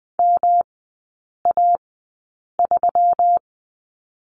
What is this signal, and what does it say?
Morse code, heard as a keyed tone, about 20 wpm, spaced as for 8 wpm, 700 Hz. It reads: MA3
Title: 152 Lesson 25e - Letter p - Three Characters 20wpm